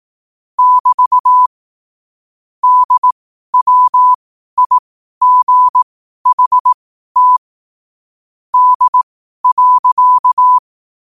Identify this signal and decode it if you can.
Morse code, heard as a keyed tone, about 18 wpm, spaced as for 12 wpm, 1000 Hz. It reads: <BT> DWIGHT D.
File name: qotdfiles / 12wpm